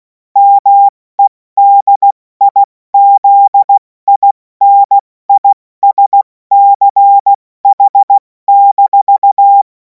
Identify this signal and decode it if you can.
Morse code, heard as a keyed tone, about 16 wpm, 800 Hz. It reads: MEDIZINISCH-
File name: Morsekurs / Witze / Witz 24. Februar 2026 70 F80